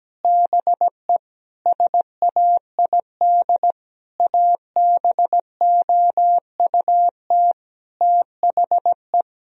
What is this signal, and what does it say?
Morse code, heard as a keyed tone, about 17 wpm, 700 Hz. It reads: BE SAID ABOUT THE
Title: Sentences from Top 1000 Words - Review Part 4 - 17wpm